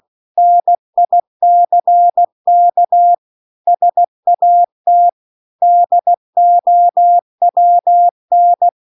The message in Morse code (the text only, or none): NICK SAT DOWN